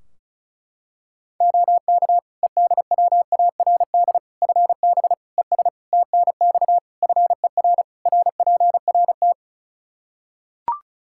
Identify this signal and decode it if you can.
Morse code, heard as a keyed tone, about 36 wpm, 700 Hz. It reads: OK EDWARD FB ES TNX FER RPRT
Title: Practice QSO - Exchange 2 - Info - 35wpm